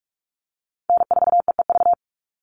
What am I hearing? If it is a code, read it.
Morse code, heard as a keyed tone, about 44 wpm, 700 Hz. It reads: N4EEV
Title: Call Signs - 45wpm